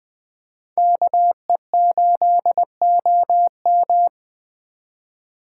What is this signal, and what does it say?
Morse code, heard as a keyed tone, about 20 wpm, 700 Hz. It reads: KE8OM
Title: Speed Racing - USA 2x2 Call Signs 17wpm